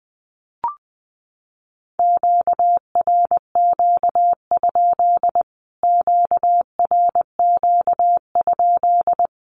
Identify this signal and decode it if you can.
Morse code, heard as a keyed tone, about 20 wpm, 700 Hz. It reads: QRQ? QRQ?
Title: Q-Codes Repeated 3x 20wpm